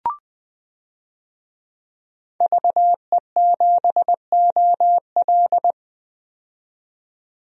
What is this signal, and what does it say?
Morse code, heard as a keyed tone, about 20 wpm, 700 Hz. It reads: VE7OL
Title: DXCC-100 Call Signs - 5 Characters - 20wpm